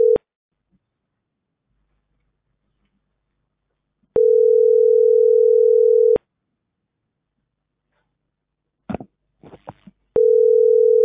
Ringback tone (North America), 440 + 480 Hz.